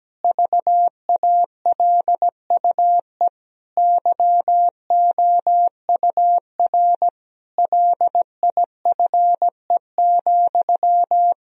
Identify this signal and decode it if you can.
Morse code, heard as a keyed tone, about 17 wpm, 700 Hz. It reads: VALUE YOUR LIFE,
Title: Sentences from Top 1000 Words - Review Part 3 - 17wpm